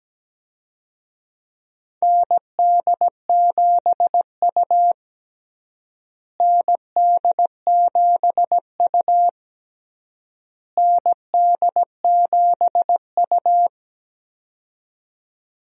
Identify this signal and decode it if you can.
Morse code, heard as a keyed tone, about 17 wpm, 700 Hz. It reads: ND7U ND7U ND7U
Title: USA Call Signs - 4 Characters - Repeated 3x - 17wpm